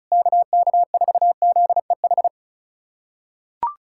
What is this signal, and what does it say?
Morse code, heard as a keyed tone, about 34 wpm, 690 Hz. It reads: KK4ZEH
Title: SOTA QSO Practice 35wpm